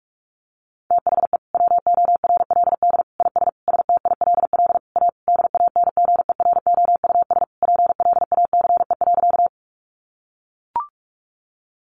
Morse code, heard as a keyed tone, about 44 wpm, 700 Hz. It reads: THE WORLD IS STILL A DANGEROUS PLACE.
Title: Sentences from Top 1200 Words 45wpm